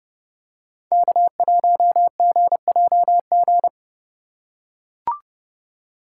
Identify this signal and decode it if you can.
Morse code, heard as a keyed tone, about 31 wpm, 700 Hz. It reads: K1GJG
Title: USA Call Signs - 5 Characters - 2x Character Spacing - 30wpm